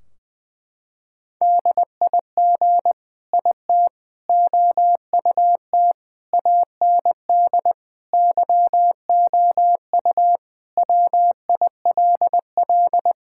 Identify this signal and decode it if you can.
Morse code, heard as a keyed tone, about 20 wpm, 700 Hz. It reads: DIG IT OUT AND YOU WILL
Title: Sentences from Top 1600 Words 20wpm